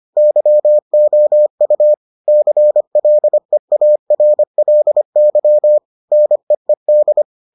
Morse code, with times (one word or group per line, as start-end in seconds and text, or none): 0.2-1.9 YOU
2.3-5.8 CLEARLY
6.1-7.2 NEED